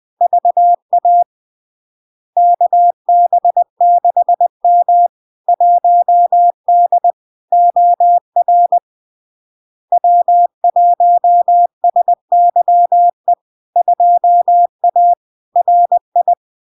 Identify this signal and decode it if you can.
Morse code, heard as a keyed tone, about 20 wpm, 700 Hz. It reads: VA KB6M 1D OR W1SYE 2A RI
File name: LARK-FD-2022-Log-in-CW-20wpm-700-Hz